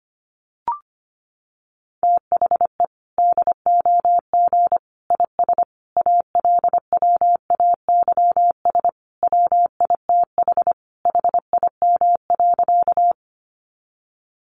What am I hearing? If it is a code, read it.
Morse code, heard as a keyed tone, about 25 wpm, 700 Hz. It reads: THE DOG IS ALWAYS WITH HIM.
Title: Sentences from Top 400 Words 25wpm